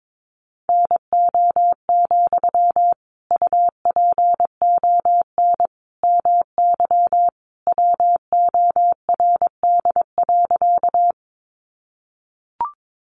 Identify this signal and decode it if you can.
Morse code, heard as a keyed tone, about 22 wpm, 700 Hz. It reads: NO, UPON MY WORD.
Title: Sentences from Top 1100 Words 22wpm